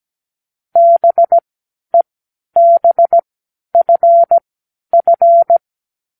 Morse code, heard as a keyed tone, about 17 wpm, spaced as for 11 wpm, 680 Hz. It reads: BEBFF
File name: k7qo / Lesson15(A-J)